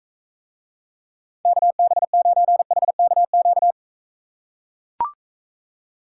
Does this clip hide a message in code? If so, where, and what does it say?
Yes: Morse code from 1.5 to 3.7 s, heard as a keyed tone, about 43 wpm, 700 Hz. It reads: KB9HKQ